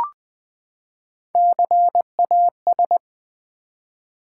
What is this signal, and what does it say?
Morse code, heard as a keyed tone, about 20 wpm, 700 Hz. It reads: CAS
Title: Build A Word - 3 Letters At A Time - 6-Letter Words - 20wpm